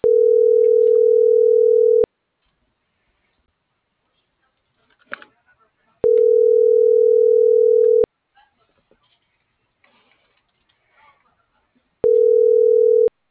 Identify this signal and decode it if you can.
Ringback tone (North America), 440 + 480 Hz.